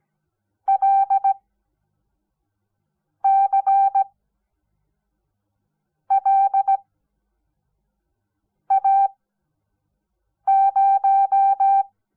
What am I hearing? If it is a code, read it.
Morse code, heard as a keyed tone, about 17 wpm, spaced as for 4 wpm, 780 Hz. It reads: LCLA0
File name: Onlinelehrgang / morsen / DJ4UF Morsen06